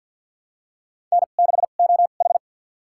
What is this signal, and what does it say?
Morse code, heard as a keyed tone, about 52 wpm, spaced as for 37 wpm, 700 Hz. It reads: N6OH
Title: USA Call Signs - 4 Characters - 2x Character Spacing - 50wpm